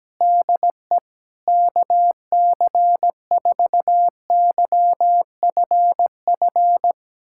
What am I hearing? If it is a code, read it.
Morse code, heard as a keyed tone, about 17 wpm, 700 Hz. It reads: DE KC4YFF
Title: USA Callsign DE Callsign - 6 Characters - 17wpm